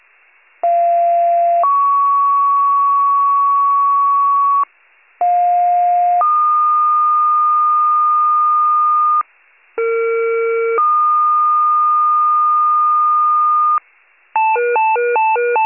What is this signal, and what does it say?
Two-tone page, heard as a one-second tone, then a three-second tone, about 690 then 1120 Hz, about 710 then 1220 Hz, about 460 then 1220 Hz. It is heard from 0.6 to 13.8 s.
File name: pager / 46 RESCUE / wcalert